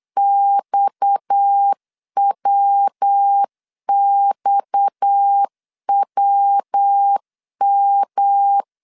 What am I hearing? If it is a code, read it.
Morse code, heard as a keyed tone, about 8 wpm, 800 Hz. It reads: XWXWM
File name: CW / DET / 1 6 1500C S B 8wpm 5142